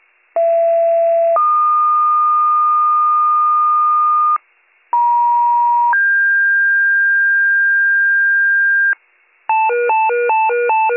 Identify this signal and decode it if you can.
Two-tone page, heard as a one-second tone, then a three-second tone, about 670 then 1220 Hz, about 950 then 1600 Hz, with noise. It is heard from 0.4 to 8.9 s.